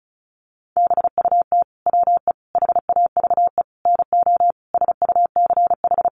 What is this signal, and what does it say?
Morse code, heard as a keyed tone, about 34 wpm, 700 Hz. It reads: BUT WE HAVE NO SUCH
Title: Sentences from Top 1300 Words 35wpm